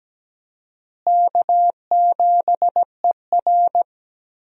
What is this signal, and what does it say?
Morse code, heard as a keyed tone, about 17 wpm, 700 Hz. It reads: K7ER
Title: USA Call Signs - 4 Characters - Repeated 2x - 17wpm